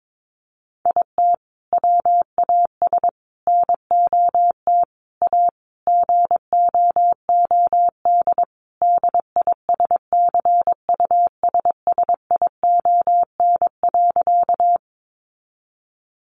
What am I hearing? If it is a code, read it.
Morse code, heard as a keyed tone, about 22 wpm, 700 Hz. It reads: IT WAS NOT A GOOD DISCUSSION.
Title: Sentences from Top 1100 Words 22wpm